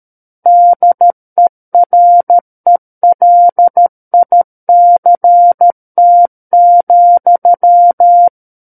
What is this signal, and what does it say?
Morse code, heard as a keyed tone, about 13 wpm, 700 Hz. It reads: DERELICT,
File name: Book One / 04-Chapter 4 - 13 WPM